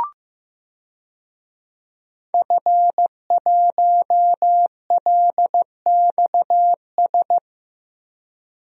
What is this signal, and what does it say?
Morse code, heard as a keyed tone, about 15 wpm, 700 Hz. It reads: F1LXS
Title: DXCC-100 Call Signs - 5 Characters - Repeated 3x - 15wpm